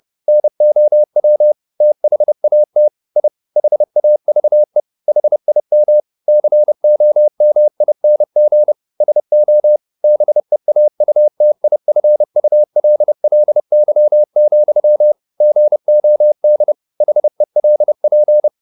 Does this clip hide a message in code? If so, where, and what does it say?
Yes: Morse code from 0.3 to 18.5 s, heard as a keyed tone, about 30 wpm, 600 Hz. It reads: NOW THAT I HAVE HIM COMING SO BEAUTIFULLY, GOD HELP